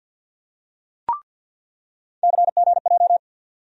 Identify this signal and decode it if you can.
Morse code, heard as a keyed tone, about 47 wpm, 700 Hz. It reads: CGJ
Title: USA Call Sign Postfixes - 3 Letters 50wpm